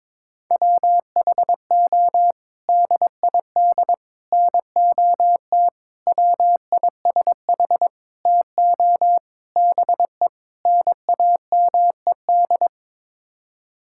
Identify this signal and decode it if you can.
Morse code, heard as a keyed tone, about 22 wpm, 700 Hz. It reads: WHO DID NOT WISH TO BE NAMED
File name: practice / 22wpm / Sets of 7 Words - 22wpm